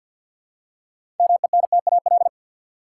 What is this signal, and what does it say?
Morse code, heard as a keyed tone, about 50 wpm, 700 Hz. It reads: MENTAL
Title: Six Letter Words - 50wpm